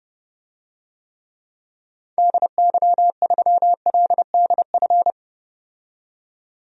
Morse code, heard as a keyed tone, about 30 wpm, 700 Hz. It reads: DY3LDF
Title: DXCC-100 Call Signs - 6 Characters - 30wpm